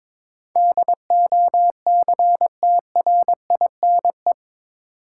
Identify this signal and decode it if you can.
Morse code, heard as a keyed tone, about 22 wpm, 700 Hz. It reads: DOCTRINE
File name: practice / 22wpm / Eight Letter Words - 22wpm